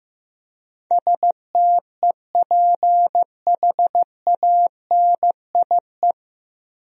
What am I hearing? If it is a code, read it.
Morse code, heard as a keyed tone, about 16 wpm, 700 Hz. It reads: STEPHANIE
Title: USA Common Names - 15wpm